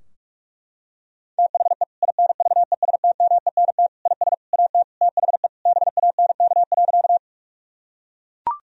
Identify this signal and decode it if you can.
Morse code, heard as a keyed tone, about 41 wpm, 700 Hz. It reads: THE INVESTMENT IS AT THE BANK.
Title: Sentences from Top 1000 Words - Review Part 4 - 45wpm